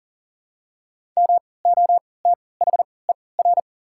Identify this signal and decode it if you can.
Morse code, heard as a keyed tone, about 40 wpm, spaced as for 23 wpm, 700 Hz. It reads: MOTHER
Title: 3x Character Spacing - Top 200 Words - 40wpm